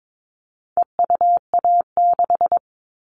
Morse code, heard as a keyed tone, about 22 wpm, 700 Hz. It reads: EUA6
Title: All Cut Numbers - 4-Digit 22wpm